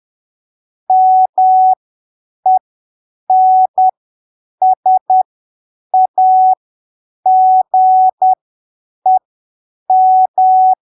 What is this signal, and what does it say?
Morse code, heard as a keyed tone, about 10 wpm, spaced as for 7 wpm, 750 Hz. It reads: MENSAGEM
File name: arquivos / texto / texto 04 10ppm